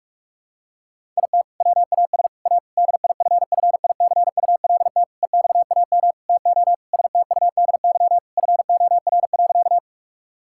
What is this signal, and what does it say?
Morse code, heard as a keyed tone, about 42 wpm, 700 Hz. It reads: IT WAS A DIFFICULT EXAM TO STUDY FOR.
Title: Sentences from Top 1000 Words - Review Part 5 - 45wpm